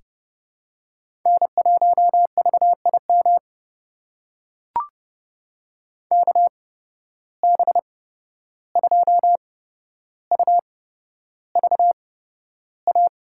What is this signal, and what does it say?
Morse code, heard as a keyed tone, about 30 wpm, spaced as for 12 wpm, 700 Hz. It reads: N1VIM KB2UVA
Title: USA Call Signs - 8x Character Spacing - 30wpm